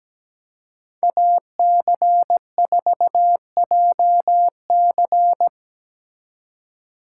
Morse code, heard as a keyed tone, about 20 wpm, 700 Hz. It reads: AC4JC
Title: Speed Racing - USA Call Signs 17wpm